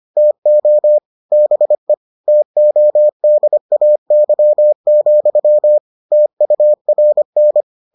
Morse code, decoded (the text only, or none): TO BE TODAY, TURN